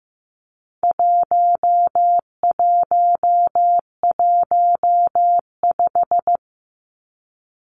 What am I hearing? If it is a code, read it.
Morse code, heard as a keyed tone, about 15 wpm, 700 Hz. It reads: 1115
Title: Numbers - 4-Digit 15wpm